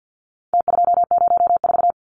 Morse code, heard as a keyed tone, about 48 wpm, 700 Hz. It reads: T204
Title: Numbers - Contest Style - 4-Digit 50wpm